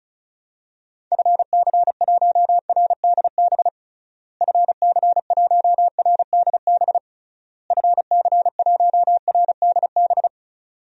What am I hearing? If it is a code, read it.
Morse code, heard as a keyed tone, about 36 wpm, 700 Hz. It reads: FC1RDB FC1RDB FC1RDB
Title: DXCC-100 Call Signs - 6 Characters - Repeated 3x - 35wpm